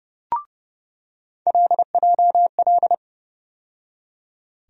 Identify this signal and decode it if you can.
Morse code, heard as a keyed tone, about 32 wpm, 700 Hz. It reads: LJL
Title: USA Call Sign Postfixes - 3 Letters 30wpm